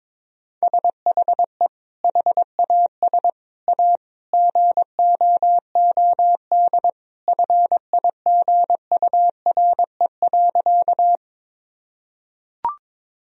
Morse code, heard as a keyed tone, about 22 wpm, 700 Hz. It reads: SHE HAS A GOOD FIGURE.
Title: Sentences from Top 1000 Words - Review Part 2 - 22wpm